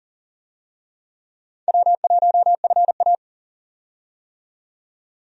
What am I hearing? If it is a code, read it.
Morse code, heard as a keyed tone, about 40 wpm, 700 Hz. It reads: W1FA
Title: USA Call Signs - 4 Characters - 40wpm